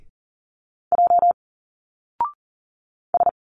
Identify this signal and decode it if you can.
Morse code, heard as a keyed tone, about 40 wpm, 700 Hz. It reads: J S
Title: Single Letters 40wpm